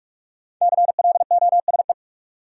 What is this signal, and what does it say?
Morse code, heard as a keyed tone, about 44 wpm, 700 Hz. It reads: CLOSE
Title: Sentences from Top 300 Words - New Word Drill - 45wpm